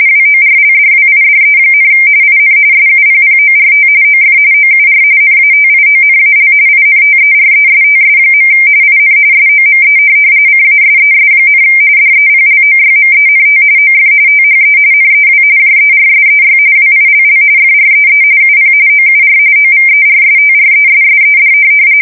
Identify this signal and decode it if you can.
RTTY, about 45.45 baud. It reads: WORLDWIDE, IN SUPPORT OF OUR NATIONAL INTERESTS. IN SOME CASES, DEPLOYMENT OF THE NEW JERSEY WILL FREE UP OUR OVERSTRESSED
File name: fsk170 / USS-New-Jersey